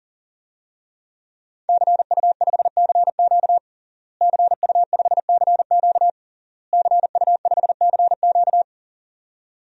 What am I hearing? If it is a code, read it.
Morse code, heard as a keyed tone, about 40 wpm, 700 Hz. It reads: CU5CQ CU5CQ CU5CQ
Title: DXCC-100 Call Signs - 5 Characters - Repeated 3x - 40wpm